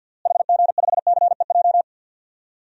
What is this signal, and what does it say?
Morse code, heard as a keyed tone, about 46 wpm, 700 Hz. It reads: HG5CEJ
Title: DXCC-100 Call Signs - 6 Characters - Repeated 3x - 50wpm